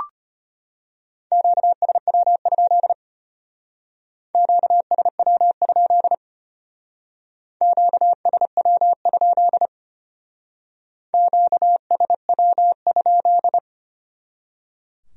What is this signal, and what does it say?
Morse code, heard as a keyed tone, about 31 wpm, 700 Hz. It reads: QSW? QSW? QSW? QSW?
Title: Speed-Racing Q-Codes - 25wpm